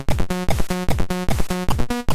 There is no code.